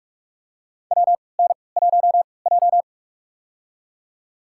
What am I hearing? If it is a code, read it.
Morse code, heard as a keyed tone, about 47 wpm, spaced as for 26 wpm, 700 Hz. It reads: WN1J
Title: USA Call Signs - 4 Characters - 3x Character Spacing - 45wpm